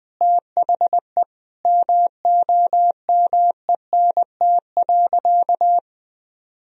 Morse code, heard as a keyed tone, about 20 wpm, 700 Hz. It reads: THE MOMENT.
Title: Sentences from Top 1000 Words - Review Part 3 - 20wpm